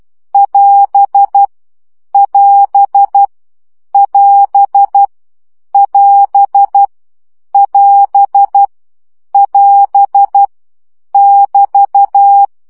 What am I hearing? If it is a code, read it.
Morse code, heard as a keyed tone, about 12 wpm, 800 Hz. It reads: <AS> <AS> <AS> <AS> <AS> <AS> <BT>
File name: CW gyakorlas as 10 baud